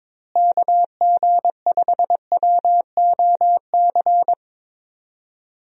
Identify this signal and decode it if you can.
Morse code, heard as a keyed tone, about 22 wpm, 700 Hz. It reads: KG5WOC
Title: USA Call Signs - 6 Characters - Repeated 3x - 22wpm